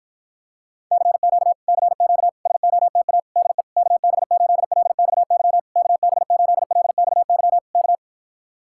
Morse code, heard as a keyed tone, about 46 wpm, 700 Hz. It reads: CQ CQ SOTA DE KB8LXY KB8LXY K